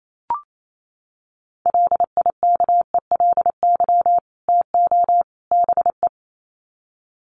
Morse code, heard as a keyed tone, about 28 wpm, 700 Hz. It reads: LIKELY TO BE
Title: Sets of 3 Words 28wpm